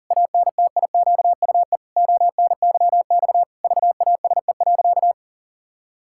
Morse code, heard as a keyed tone, about 40 wpm, 700 Hz. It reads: ANTIQUE ONYX VASE.
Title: Pangram Challenge v2 - 40wpm